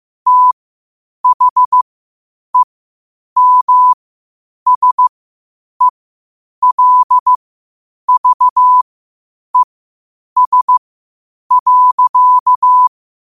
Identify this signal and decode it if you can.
Morse code, heard as a keyed tone, about 15 wpm, spaced as for 8 wpm, 1000 Hz. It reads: THEMSELVES.
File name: qotdfiles / 8wpm